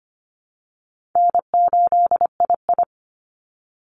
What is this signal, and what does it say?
Morse code, heard as a keyed tone, about 25 wpm, 700 Hz. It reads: N8II
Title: Easy Call Signs Repeated 2x - 25wpm